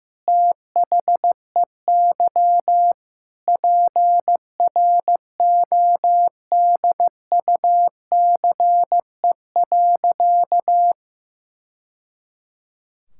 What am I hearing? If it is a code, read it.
Morse code, heard as a keyed tone, about 15 wpm, 700 Hz. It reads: THEY PRODUCE.
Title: Sentences from Top 1000 Words - Review Part 5 - 15wpm